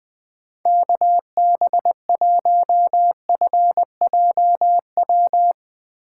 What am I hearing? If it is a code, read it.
Morse code, heard as a keyed tone, about 20 wpm, 700 Hz. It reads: KB1FJW
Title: USA Call Signs - 6 Characters - 2x Character Spacing - 20wpm